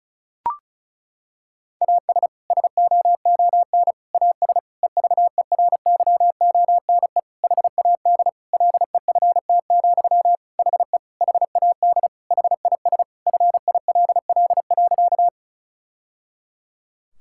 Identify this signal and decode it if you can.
Morse code, heard as a keyed tone, about 36 wpm, 700 Hz. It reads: AS SOON AS EVERYONE HAD LEFT, HE HAD HIS FILL.
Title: Sentences from Top 1000 Words - Review Part 5 - 35wpm